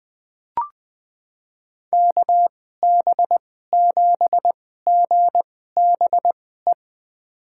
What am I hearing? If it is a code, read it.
Morse code, heard as a keyed tone, about 20 wpm, spaced as for 14 wpm, 700 Hz. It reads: KB7GBE
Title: USA Call Signs - 6 Characters - 2x Character Spacing - 20wpm